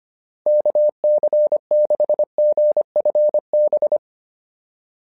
Morse code, heard as a keyed tone, about 25 wpm, 590 Hz. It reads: KC6GFB
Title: SOTA QSO Practice 25wpm